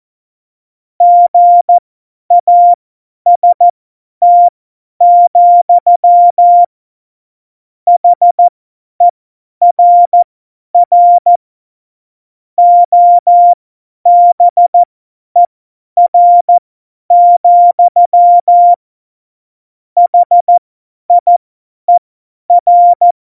Morse code, heard as a keyed tone, about 14 wpm, spaced as for 10 wpm, 700 Hz. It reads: GAST, HERR OBER, HIER
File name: Morsekurs / Witze / Witz 25. Juni 2019, 50 F70